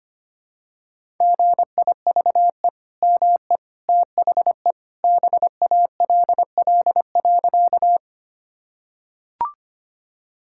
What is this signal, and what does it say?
Morse code, heard as a keyed tone, about 25 wpm, 700 Hz. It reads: GIVE ME THE BALL.
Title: Sentences from Top 1000 Words - Review Part 6 - 25wpm